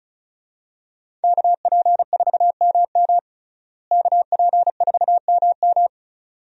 Morse code, heard as a keyed tone, about 36 wpm, 700 Hz. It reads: KP4MM KP4MM
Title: USA Call Signs - 5 Characters - Repeated 2x - 35wpm